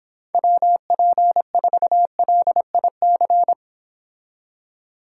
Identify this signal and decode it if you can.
Morse code, heard as a keyed tone, about 26 wpm, 700 Hz. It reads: WP4LIC